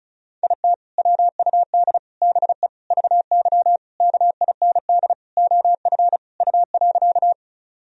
Morse code, heard as a keyed tone, about 36 wpm, 700 Hz. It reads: IT WUD BE VY KIND OF U.
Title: Sentences from Top 200 Words and Abbreviations 35wpm